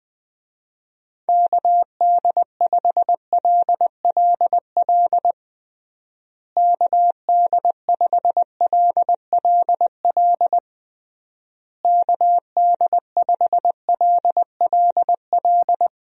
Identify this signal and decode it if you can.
Morse code, heard as a keyed tone, about 21 wpm, 700 Hz. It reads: KD5LLL KD5LLL KD5LLL
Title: USA Call Signs - 6 Characters - Repeated 3x - 20wpm